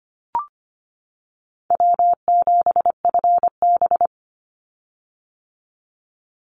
Morse code, heard as a keyed tone, about 25 wpm, 700 Hz. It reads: W7FB
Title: Call Signs - 25wpm